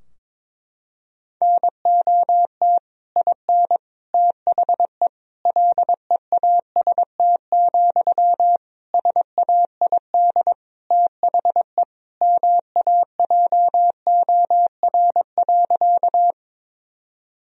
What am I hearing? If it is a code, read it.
Morse code, heard as a keyed tone, about 21 wpm, 700 Hz. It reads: NOT IN THE LEAST, SAID THE MAJOR.
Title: Sentences from Top 1000 Words - Review Part 4 - 22wpm